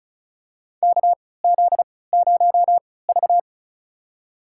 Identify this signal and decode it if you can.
Morse code, heard as a keyed tone, about 34 wpm, spaced as for 20 wpm, 700 Hz. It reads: KZ0V
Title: USA Call Signs - 4 Characters - 3x Character Spacing - 35wpm